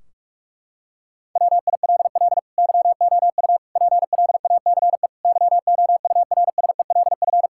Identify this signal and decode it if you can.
Morse code, heard as a keyed tone, about 46 wpm, 700 Hz. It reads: WILL YOU PLACE YOURSELF
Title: Sentences from Top 1600 Words 45wpm